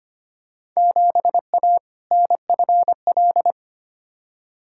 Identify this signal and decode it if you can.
Morse code, heard as a keyed tone, about 25 wpm, 700 Hz. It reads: 7A NFL
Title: Field Day Exchanges Encore - 25wpm